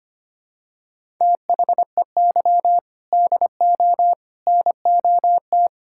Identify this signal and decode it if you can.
Morse code, heard as a keyed tone, about 25 wpm, 700 Hz. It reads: THEY DO NOT
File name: practice / 25wpm / Sentences with 3-Word Phrases - 25wpm